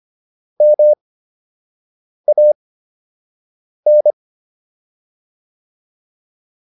Morse code, heard as a keyed tone, about 25 wpm, spaced as for 6 wpm, 600 Hz. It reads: MAN